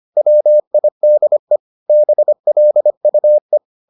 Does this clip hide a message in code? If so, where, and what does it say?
Yes: Morse code from 0.2 to 3.6 s, heard as a keyed tone, about 25 wpm, 600 Hz. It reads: WIDE BLUE